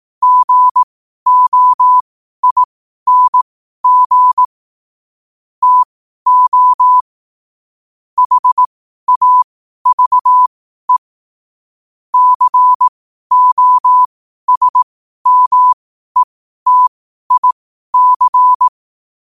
Morse code, heard as a keyed tone, about 18 wpm, spaced as for 12 wpm, 1000 Hz. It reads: GOING TO HAVE COSMETIC
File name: qotdfiles / 12wpm